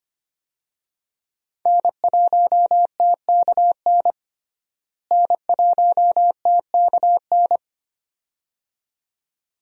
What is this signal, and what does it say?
Morse code, heard as a keyed tone, about 25 wpm, 700 Hz. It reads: N1TKN N1TKN
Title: USA Call Signs - 5 Characters - Repeated 2x - 25wpm